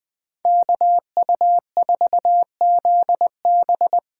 Morse code, heard as a keyed tone, about 21 wpm, 700 Hz. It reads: KU4ZB
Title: Speed Racing - USA 2x2 Call Signs 17wpm